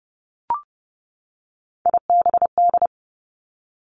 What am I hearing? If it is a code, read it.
Morse code, heard as a keyed tone, about 30 wpm, 700 Hz. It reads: IBD
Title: Three Challenging Letter-Number Combinations 30wpm